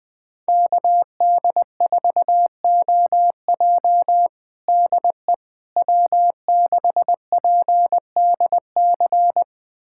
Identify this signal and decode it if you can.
Morse code, heard as a keyed tone, about 20 wpm, 700 Hz. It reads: KD4OJ DE W6PDC
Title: USA Callsign DE Callsign - 5 Characters - 20wpm